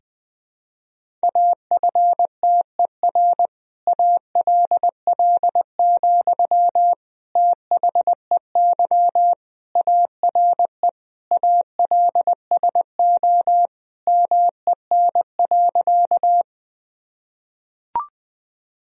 Morse code, heard as a keyed tone, about 21 wpm, 700 Hz. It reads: AFTER ALL, THEY ARE ALSO MEN.
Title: Sentences from Top 1000 Words - Review Part 1 - 20wpm